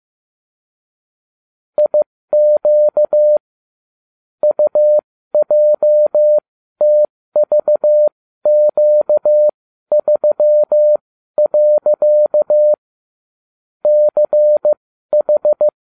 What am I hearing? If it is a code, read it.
Morse code, heard as a keyed tone, about 15 wpm, spaced as for 12 wpm, 600 Hz. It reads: IQ UJTVQ3. CH